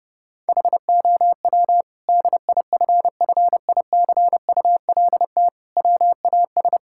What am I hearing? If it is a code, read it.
Morse code, heard as a keyed tone, about 30 wpm, 700 Hz. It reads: HOW DIFFICULT WAS
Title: Sentences from Top 1000 Words - Review Part 6 - 30wpm